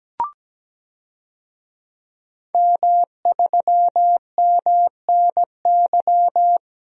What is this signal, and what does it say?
Morse code, heard as a keyed tone, about 17 wpm, 700 Hz. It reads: M3MNY
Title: DXCC-100 Call Signs - 5 Characters - Repeated 3x - 17wpm